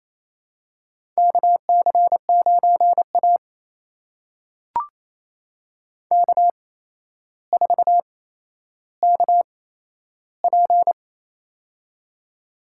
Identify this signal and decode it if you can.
Morse code, heard as a keyed tone, about 28 wpm, spaced as for 12 wpm, 700 Hz. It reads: KC9A K4KP
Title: USA Call Signs - 4 Characters - 8x Character Spacing - 28wpm